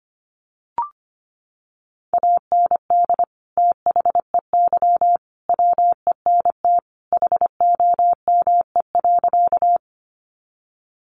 Morse code, heard as a keyed tone, about 25 wpm, 700 Hz. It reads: AND THEY WENT HOME.
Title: Sentences from Top 200 Words 25wpm